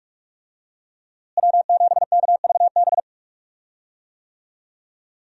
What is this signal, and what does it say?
Morse code, heard as a keyed tone, about 46 wpm, 700 Hz. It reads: W7KVB
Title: USA Call Signs - 5 Characters - Repeated 3x - 45wpm